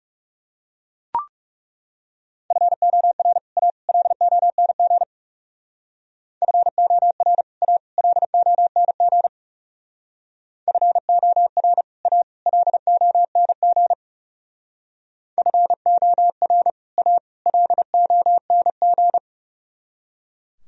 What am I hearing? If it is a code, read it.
Morse code, heard as a keyed tone, about 37 wpm, 700 Hz. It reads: FOR A LONG FOR A LONG FOR A LONG FOR A LONG
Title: Speed-Racing - Sets of 3 Words 30wpm